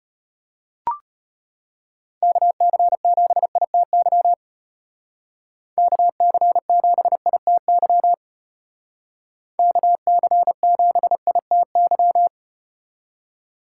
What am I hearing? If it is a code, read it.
Morse code, heard as a keyed tone, about 34 wpm, 700 Hz. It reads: KC7ITY KC7ITY KC7ITY